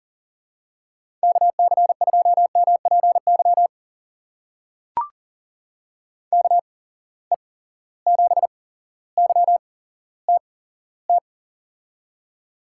Morse code, heard as a keyed tone, about 40 wpm, spaced as for 17 wpm, 700 Hz. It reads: KC2MPY KE7YTT
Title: USA Call Signs - 6 Characters - 8x Character Spacing - 40wpm